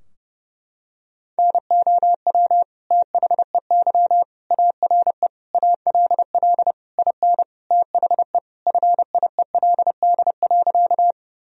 Morse code, heard as a keyed tone, about 30 wpm, 700 Hz. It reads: NOW THEY ARE ALL IN THE FIELD.
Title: Sentences from Top 1000 Words - Review Part 2 - 30wpm